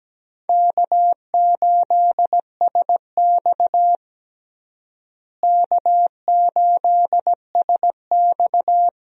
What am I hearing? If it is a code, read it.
Morse code, heard as a keyed tone, about 17 wpm, 700 Hz. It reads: K8SX K8SX
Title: USA Call Signs - 4 Characters - Repeated 3x - 17wpm